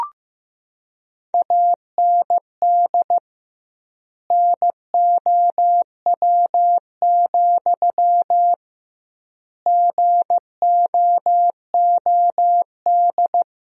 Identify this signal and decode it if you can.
Morse code, heard as a keyed tone, about 15 wpm, 700 Hz. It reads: AND NOW, GOOD
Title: 2x Word-Spacing - Sentences from Top 300 Words - 15wpm